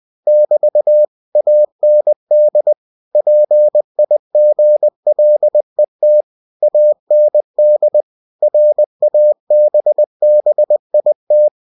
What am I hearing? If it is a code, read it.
Morse code, heard as a keyed tone, about 20 wpm, 600 Hz. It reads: <BT> AND PIGLET AND RABBIT
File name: cw-books / winnie-the-pooh / winnie-20wpm-page-0011